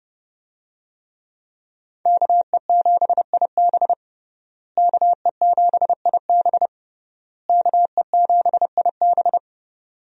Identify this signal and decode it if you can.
Morse code, heard as a keyed tone, about 31 wpm, 700 Hz. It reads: KE7IB KE7IB KE7IB
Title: USA Call Signs - 5 Characters - Repeated 3x - 30wpm